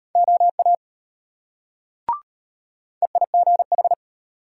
Morse code, heard as a keyed tone, about 33 wpm, 700 Hz. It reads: OA EIGH